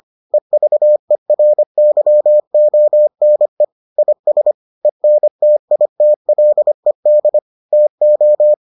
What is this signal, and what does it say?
Morse code, heard as a keyed tone, about 25 wpm, 600 Hz. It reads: EVERYONE IS ENTITLED TO